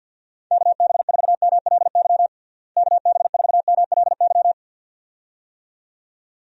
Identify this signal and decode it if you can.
Morse code, heard as a keyed tone, about 52 wpm, 700 Hz. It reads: KB4MLY KB4MLY
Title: USA Call Signs - 6 Characters - Repeated 2x - 50wpm